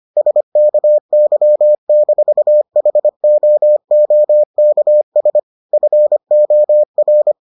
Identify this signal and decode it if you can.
Morse code, heard as a keyed tone, about 25 wpm, 600 Hz. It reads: SKY-HOOKS FOR